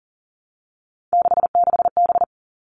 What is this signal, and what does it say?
Morse code, heard as a keyed tone, about 40 wpm, 700 Hz. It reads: 66B
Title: All Cut Numbers - 3-Digit 40wpm